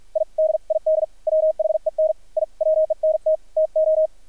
Morse code, heard as a keyed tone, about 53 wpm, spaced as for 45 wpm, 630 Hz, with radio noise. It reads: I DID WHEN I WENT TO
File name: archived / CRSnet